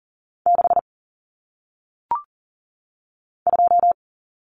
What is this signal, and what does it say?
Morse code, heard as a keyed tone, about 39 wpm, 700 Hz. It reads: 6 2
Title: Numbers - 1-Digit 40wpm